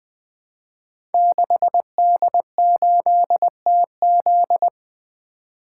Morse code, heard as a keyed tone, about 21 wpm, 700 Hz. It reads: 6D8TZ
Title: Code Groups - 5 Letter:Numbers - 20wpm